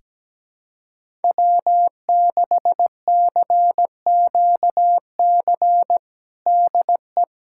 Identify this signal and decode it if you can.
Morse code, heard as a keyed tone, about 17 wpm, 700 Hz. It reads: W6CQC DE
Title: USA Callsign DE Callsign 17wpm